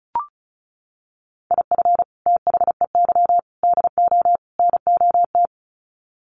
Morse code, heard as a keyed tone, about 34 wpm, 700 Hz. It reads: IF THEY DO NOT
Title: Sets of 4 Words 35wpm